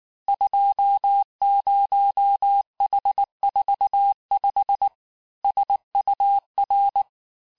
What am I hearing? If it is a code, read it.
Morse code, heard as a keyed tone, about 19 wpm, 800 Hz. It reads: 20H45 SUR
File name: cours-du-lundi files / cours-1589-texte-chiffre-vitesse-18mpm